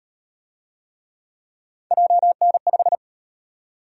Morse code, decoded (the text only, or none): JN5